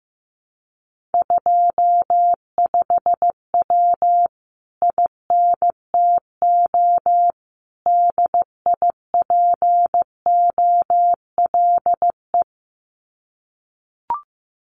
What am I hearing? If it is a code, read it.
Morse code, heard as a keyed tone, about 15 wpm, 700 Hz. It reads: 25W INTO DIPOLE
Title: Common QSO Phrases Encore 15wpm